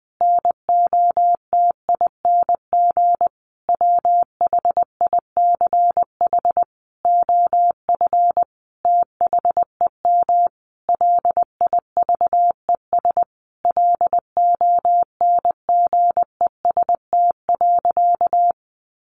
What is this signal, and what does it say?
Morse code, heard as a keyed tone, about 20 wpm, 700 Hz. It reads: NOTING WHICH OF THEM LIVES LONGEST.
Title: A Sound Of Thunder 20wpm